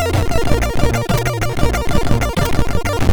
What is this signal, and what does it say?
There is no code.